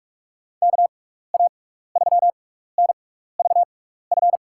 Morse code, heard as a keyed tone, about 48 wpm, spaced as for 16 wpm, 700 Hz. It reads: KA3NVF
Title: USA Call Signs - 6 Characters - 6x Character Spacing - 45wpm